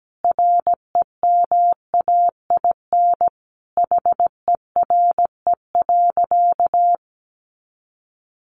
Morse code, heard as a keyed tone, about 17 wpm, 700 Hz. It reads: REMAIN HERE.
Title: Sentences from Top 1200 Words 17wpm